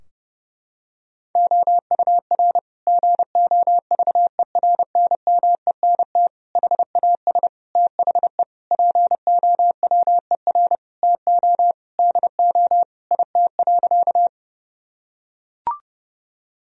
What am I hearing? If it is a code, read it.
Morse code, heard as a keyed tone, about 32 wpm, 700 Hz. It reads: OUR GOVERNMENT HAS THE POWER TO DO IT.
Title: Sentences from Top 1000 Words - Review Part 3 - 30wpm